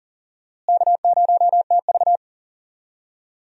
Morse code, heard as a keyed tone, about 40 wpm, 700 Hz. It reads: K0TV
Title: USA Call Signs - 4 Characters - Repeated 3x - 40wpm